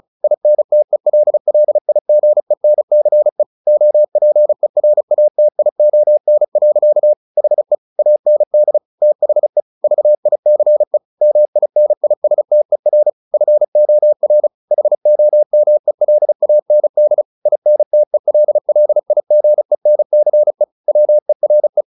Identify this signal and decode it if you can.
Morse code, heard as a keyed tone, about 34 wpm, 600 Hz. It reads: INTELLIGENCE OPERATION. HE AND THE VICE MINISTER FOR HOMELAND INTELLIGENCE WERE